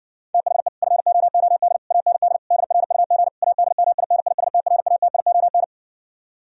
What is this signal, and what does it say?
Morse code, heard as a keyed tone, about 69 wpm, 700 Hz. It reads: THE FOOD AND DRUG ADMINISTRATION